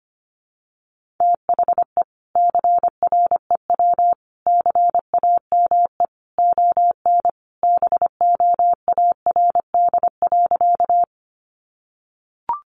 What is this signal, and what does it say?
Morse code, heard as a keyed tone, about 25 wpm, 700 Hz. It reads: THE CREW CAME ON BOARD.
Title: Sentences from Top 1500 Words 25wpm